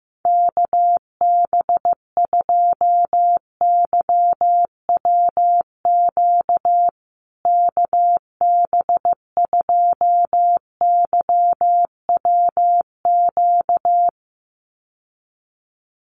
Morse code, heard as a keyed tone, about 15 wpm, 700 Hz. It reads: KB2YWQ KB2YWQ
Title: USA Call Signs Repeated 2x 15wpm